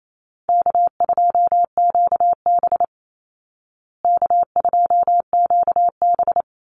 Morse code, heard as a keyed tone, about 28 wpm, 700 Hz. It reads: K2QB K2QB
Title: Call Signs Repeated 2x - 28wpm